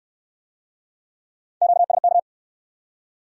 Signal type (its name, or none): Morse code